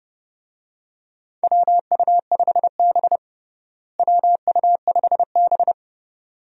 Morse code, heard as a keyed tone, about 30 wpm, 700 Hz. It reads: WU5B WU5B
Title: USA Call Signs - 4 Characters - Repeated 2x - 30wpm